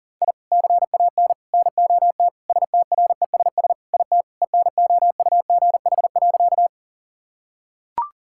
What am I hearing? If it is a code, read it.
Morse code, heard as a keyed tone, about 40 wpm, 690 Hz. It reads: I CAN NOT STRESS IT ENOUGH.
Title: Sentences from Top 1000 Words - Review Part 6 - 40wpm